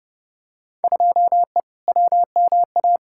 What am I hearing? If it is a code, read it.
Morse code, heard as a keyed tone, about 30 wpm, 700 Hz. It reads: 2E WMA
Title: Field Day Exchanges Encore - 30wpm